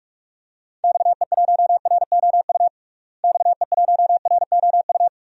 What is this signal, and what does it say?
Morse code, heard as a keyed tone, about 46 wpm, 700 Hz. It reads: XE1ROU XE1ROU
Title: DXCC-100 Call Signs - 6 Characters - Repeated 3x - 45wpm